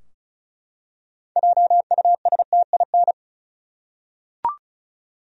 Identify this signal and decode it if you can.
Morse code, heard as a keyed tone, about 36 wpm, 700 Hz. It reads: JUSTIN
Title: USA Common Names - 35wpm